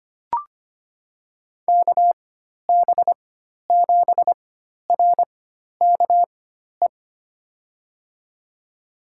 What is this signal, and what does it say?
Morse code, heard as a keyed tone, about 25 wpm, spaced as for 12 wpm, 700 Hz. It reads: KB7RKE
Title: USA Call Signs - 6 Characters - 4x Character Spacing - 25wpm